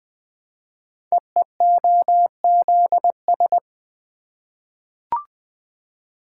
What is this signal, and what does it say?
Morse code, heard as a keyed tone, about 20 wpm, 700 Hz. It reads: EEOZS
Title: Code Groups - 5 Letters - 20wpm